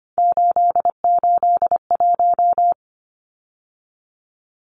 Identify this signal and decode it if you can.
Morse code, heard as a keyed tone, about 25 wpm, 700 Hz. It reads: 881
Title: Numbers - 3-Digit 25wpm